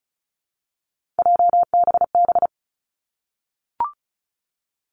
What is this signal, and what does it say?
Morse code, heard as a keyed tone, about 36 wpm, 700 Hz. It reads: JBB
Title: Three Challenging Letter-Number Combinations 35wpm